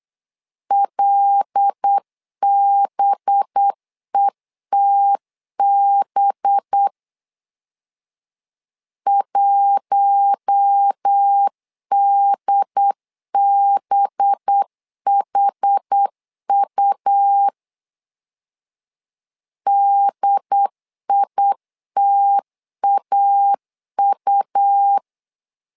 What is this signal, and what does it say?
Morse code, heard as a keyed tone, about 8 wpm, 800 Hz. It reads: LBETB 1DBHU DITAU
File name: LEZIONI / 5 LBVC / 5 LB 8wpm 2816